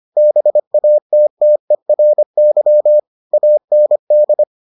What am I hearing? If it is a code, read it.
Morse code, heard as a keyed tone, about 25 wpm, 600 Hz. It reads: BATTERY AND